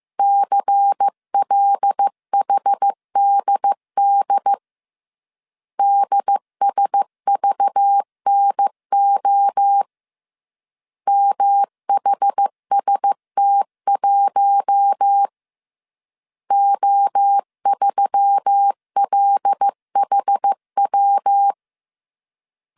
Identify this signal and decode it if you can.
Morse code, heard as a keyed tone, about 15 wpm, 800 Hz. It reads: CLHDD DSVNO MHST1 O3LHW
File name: CW / DET / 1 6 1500C S C 14wpm 2951